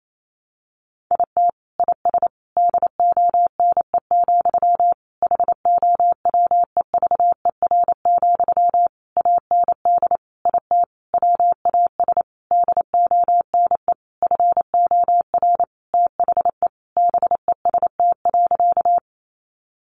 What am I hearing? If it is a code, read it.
Morse code, heard as a keyed tone, about 28 wpm, 700 Hz. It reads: IT IS DONE, HOWEVER, AND IT WAS DONE FOR THE BEST.
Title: Sentences from Top 600 Words 28wpm